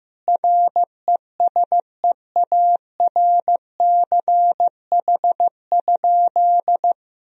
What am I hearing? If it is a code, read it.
Morse code, heard as a keyed tone, about 15 wpm, 700 Hz. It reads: RESEARCH?
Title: Sentences from Top 1000 Words - Review Part 4 - 15wpm